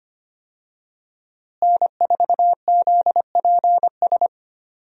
Morse code, heard as a keyed tone, about 25 wpm, 700 Hz. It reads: N4ZPS
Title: USA Call Signs - 5 Characters - Repeated 3x - 25wpm